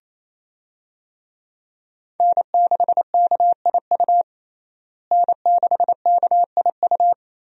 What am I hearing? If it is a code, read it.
Morse code, heard as a keyed tone, about 28 wpm, 700 Hz. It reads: N6KIU N6KIU
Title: USA Call Signs - 5 Characters - Repeated 3x - 28wpm